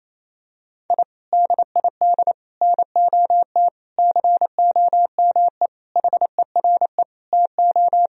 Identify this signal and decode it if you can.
Morse code, heard as a keyed tone, about 28 wpm, 700 Hz. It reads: I DID NOT COME HERE TO
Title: Sentences from Top 1000 Words - Review Part 1 - 28wpm